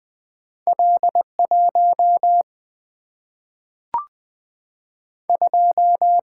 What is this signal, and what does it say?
Morse code, heard as a keyed tone, about 20 wpm, 700 Hz. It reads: L 1 2
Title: Single-Letter Single-Number 20wpm